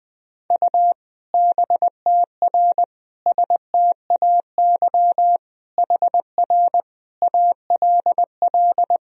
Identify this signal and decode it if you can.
Morse code, heard as a keyed tone, about 21 wpm, 700 Hz. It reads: U BTR STAY HR ALL
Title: Sentences from Top 500 Words and Abbreviations 20wpm